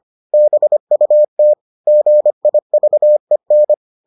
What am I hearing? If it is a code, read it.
Morse code, heard as a keyed tone, about 25 wpm, 600 Hz. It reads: BUT GIVEN